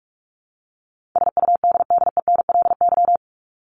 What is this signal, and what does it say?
Morse code, heard as a keyed tone, about 44 wpm, 700 Hz. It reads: SUDDENLY
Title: LY Suffix - Top 100 Words with Base Words 45wpm